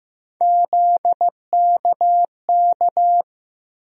Morse code, heard as a keyed tone, about 15 wpm, 700 Hz. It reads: ZKK
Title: USA Call Sign Postfixes - 3 Letters 15wpm